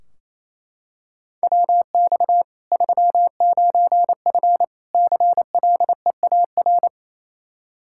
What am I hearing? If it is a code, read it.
Morse code, heard as a keyed tone, about 28 wpm, 700 Hz. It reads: WX 39F CLEAR
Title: Practice QSO - Exchange 2 - Info - 28wpm